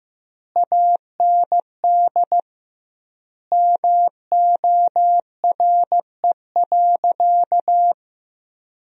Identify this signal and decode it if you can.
Morse code, heard as a keyed tone, about 15 wpm, 700 Hz. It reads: AND MORE.
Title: 2x Word-Spacing - Sentences from Top 500 Words - 15wpm